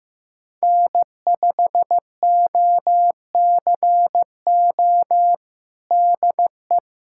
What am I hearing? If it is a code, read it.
Morse code, heard as a keyed tone, about 15 wpm, 700 Hz. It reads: N5OCO DE
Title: USA Callsign DE Callsign 15wpm